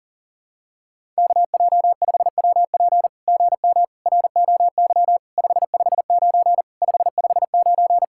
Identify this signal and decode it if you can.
Morse code, heard as a keyed tone, about 40 wpm, 700 Hz. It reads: KJ5WP GM ROY 559 559
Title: SOTA QSO Practice 40wpm